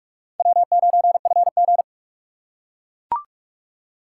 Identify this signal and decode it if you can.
Morse code, heard as a keyed tone, about 43 wpm, 700 Hz. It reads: W9FG